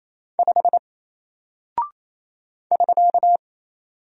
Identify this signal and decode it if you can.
Morse code, heard as a keyed tone, about 28 wpm, 700 Hz. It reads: 5 <SK>
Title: Single Character (Letter:Number:Punctuation:Prosign) Alt 28wpm